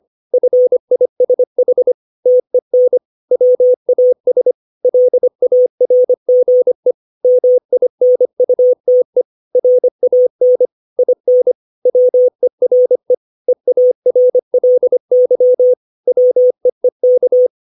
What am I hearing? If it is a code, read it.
Morse code, heard as a keyed tone, about 25 wpm, 500 Hz. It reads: FISH TEN WAS LARGE MINUTE RAN IN WERE EARLY WEEK
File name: 632 words / 632 Words 25WPM